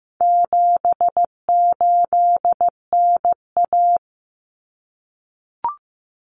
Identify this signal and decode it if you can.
Morse code, heard as a keyed tone, about 15 wpm, 700 Hz. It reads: 78NA
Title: Common Cut Numbers - 4-Digit 15wpm